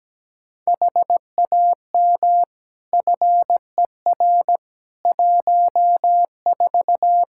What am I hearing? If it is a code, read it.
Morse code, heard as a keyed tone, about 17 wpm, 700 Hz. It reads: HAM FER 14
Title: Practice QSO - Exchange 3 - More Info - 17wpm